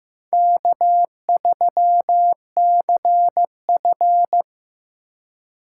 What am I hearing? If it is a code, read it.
Morse code, heard as a keyed tone, about 15 wpm, 700 Hz. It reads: K3CF
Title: USA Call Signs - 4 Characters - 2x Character Spacing - 15wpm